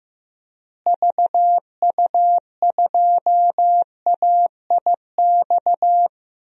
Morse code, heard as a keyed tone, about 15 wpm, 700 Hz. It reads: VU2AIX
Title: DXCC-100 Call Signs - 6 Characters - Repeated 2x - 15wpm